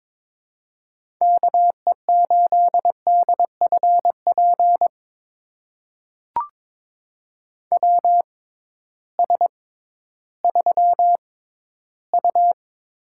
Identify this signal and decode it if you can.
Morse code, heard as a keyed tone, about 22 wpm, spaced as for 13 wpm, 700 Hz. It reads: KE8DFP WS3U
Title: USA Call Signs - 6x Character Spacing - 22wpm